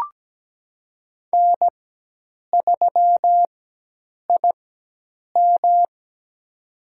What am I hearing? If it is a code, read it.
Morse code, heard as a keyed tone, about 17 wpm, spaced as for 8 wpm, 700 Hz. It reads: N3IM
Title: USA Call Signs - 4 Characters - 4x Character Spacing - 17wpm